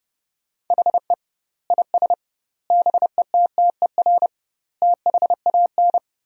Morse code, heard as a keyed tone, about 30 wpm, 700 Hz. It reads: HE IS BETTER THAN
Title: 2x Word-Spacing - Sentences from Top 400 Words - 30wpm